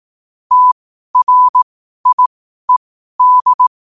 Morse code, heard as a keyed tone, about 18 wpm, spaced as for 12 wpm, 1000 Hz. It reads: TRIED
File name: qotdfiles / 12wpm